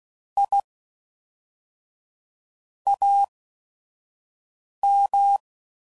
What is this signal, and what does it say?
Morse code, heard as a keyed tone, about 16 wpm, spaced as for 5 wpm, 800 Hz. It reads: I AM
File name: CW Code Groups @5WPM / QSOs / WD0FBT 2